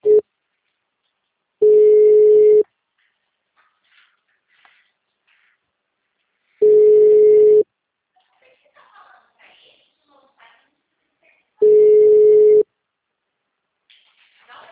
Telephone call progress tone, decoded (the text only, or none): ringback tone